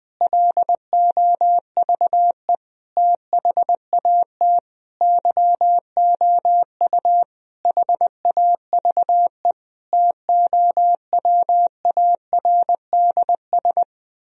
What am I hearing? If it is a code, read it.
Morse code, heard as a keyed tone, about 20 wpm, 700 Hz. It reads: LOVE THAT YOU HAVE TOWARDS
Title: Sentences from Top 1000 Words - Review Part 4 - 20wpm